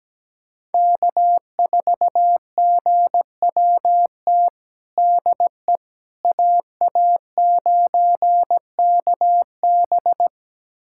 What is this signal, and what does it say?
Morse code, heard as a keyed tone, about 17 wpm, 700 Hz. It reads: K4GWT DE AA9KB
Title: USA Callsign DE Callsign 17wpm